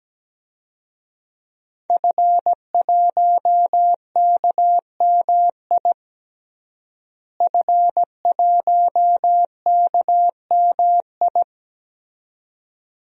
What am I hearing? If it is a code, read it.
Morse code, heard as a keyed tone, about 17 wpm, 700 Hz. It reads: F1KMI F1KMI
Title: DXCC-100 Call Signs - 5 Characters - Repeated 2x - 17wpm